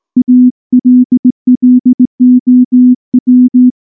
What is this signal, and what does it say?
Morse code, heard as a keyed tone, about 19 wpm, 250 Hz. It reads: ALLOW
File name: morse / morse-training-20WPM-9